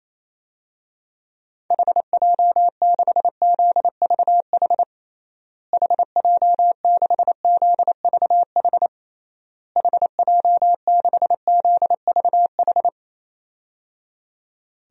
Morse code, heard as a keyed tone, about 27 wpm, 700 Hz. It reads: HJ6ZVH HJ6ZVH HJ6ZVH
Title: DXCC-100 Call Signs - 6 Characters - Repeated 3x - 28wpm